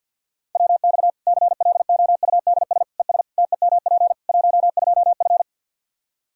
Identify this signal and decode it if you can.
Morse code, heard as a keyed tone, about 51 wpm, 700 Hz. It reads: WX CLOUDS ES TEMP 12F
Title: K1USN SST QSO Practice 50wpm